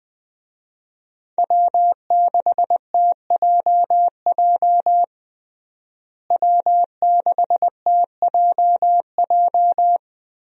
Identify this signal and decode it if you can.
Morse code, heard as a keyed tone, about 20 wpm, 700 Hz. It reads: W6TJJ W6TJJ
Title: USA Call Signs - 5 Characters - Repeated 3x - 20wpm